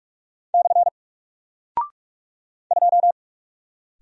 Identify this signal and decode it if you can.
Morse code, heard as a keyed tone, about 45 wpm, 700 Hz. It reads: / 2
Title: Single Character (Letter:Number:Punctuation:Prosign) 45wpm